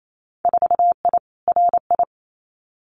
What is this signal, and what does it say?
Morse code, heard as a keyed tone, about 28 wpm, 700 Hz. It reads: 4I RI
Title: Winter Field Day Exchanges Encore 28wpm